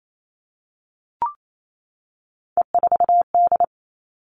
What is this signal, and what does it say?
Morse code, heard as a keyed tone, about 28 wpm, 700 Hz. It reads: E4D
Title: All Cut Numbers - 3-Digit 28wpm